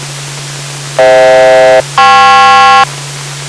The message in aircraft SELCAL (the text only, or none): FH-KP